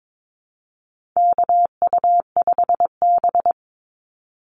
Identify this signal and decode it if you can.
Morse code, heard as a keyed tone, about 22 wpm, 700 Hz. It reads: KU5B
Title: Call Signs Repeated 2x - 22wpm